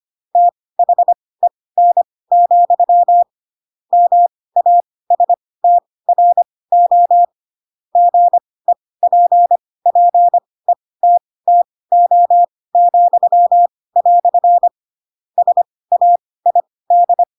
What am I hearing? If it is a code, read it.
Morse code, heard as a keyed tone, about 25 wpm, spaced as for 18 wpm, 700 Hz. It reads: THEN, MASTRO GEPPETTO," SAID
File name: Pinocchio / Pinocchio ch2 12 25 ik1hge